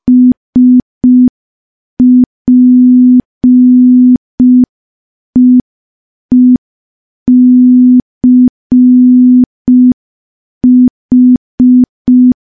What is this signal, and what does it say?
Morse code, heard as a keyed tone, about 5 wpm, 250 Hz. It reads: SPEECH
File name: morse / morse-training-05WPM